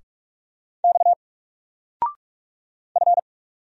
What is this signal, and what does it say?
Morse code, heard as a keyed tone, about 44 wpm, 700 Hz. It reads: X F
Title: Single Character (Letter:Number:Punctuation:Prosign) 45wpm